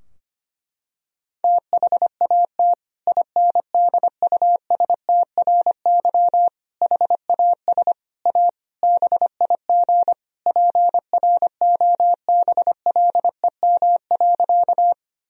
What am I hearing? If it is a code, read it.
Morse code, heard as a keyed tone, about 25 wpm, 700 Hz. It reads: THAT INDUSTRY HAS A BIG PROBLEM.
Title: Sentences from Top 1000 Words - Review Part 4 - 25wpm